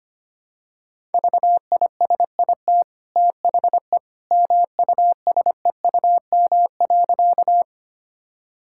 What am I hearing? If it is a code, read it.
Morse code, heard as a keyed tone, about 25 wpm, 700 Hz. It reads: VISIT THE MUSEUM.
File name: Easy Sentences with Hints - 25wpm